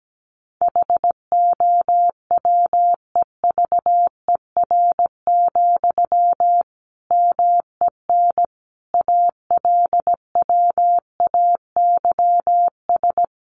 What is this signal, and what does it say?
Morse code, heard as a keyed tone, about 17 wpm, 700 Hz. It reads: HOWEVER, MEN ALWAYS
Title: Sentences from Top 600 Words 17wpm